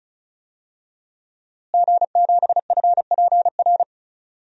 Morse code, heard as a keyed tone, about 36 wpm, 700 Hz. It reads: G7FPR
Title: DXCC-100 Call Signs - 5 Characters - Repeated 2x - 35wpm